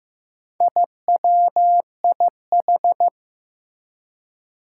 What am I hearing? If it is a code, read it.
Morse code, heard as a keyed tone, about 16 wpm, 700 Hz. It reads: IWIH
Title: Four Challenging Letter-Number Combinations 15wpm